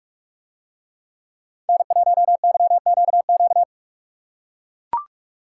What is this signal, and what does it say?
Morse code, heard as a keyed tone, about 44 wpm, 700 Hz. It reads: N1YQQ